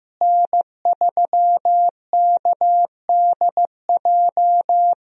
Morse code, heard as a keyed tone, about 16 wpm, 700 Hz. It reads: N3KDJ
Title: USA Call Signs - 5 Characters - Repeated 2x - 15wpm